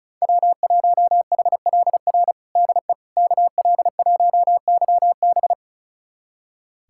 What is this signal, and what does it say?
Morse code, heard as a keyed tone, about 34 wpm, 700 Hz. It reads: W1HLR DE KL1YB
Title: USA Callsign DE Callsign - 5 Characters - 35wpm